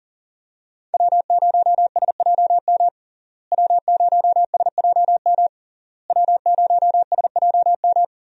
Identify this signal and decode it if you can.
Morse code, heard as a keyed tone, about 40 wpm, 700 Hz. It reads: W0SJM W0SJM W0SJM
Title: USA Call Signs - 5 Characters - Repeated 3x - 40wpm